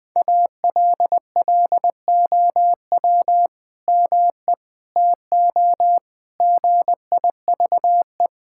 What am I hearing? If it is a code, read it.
Morse code, heard as a keyed tone, about 20 wpm, 700 Hz. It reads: ALLOW ME TO GIVE
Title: Sentences from Top 1000 Words - Review Part 5 - 20wpm